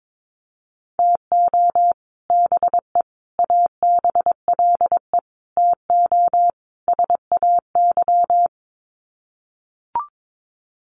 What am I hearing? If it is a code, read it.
Morse code, heard as a keyed tone, about 22 wpm, 700 Hz. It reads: TO BE ABLE TO SAY
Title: Sets of 5 Words Encore2 22wpm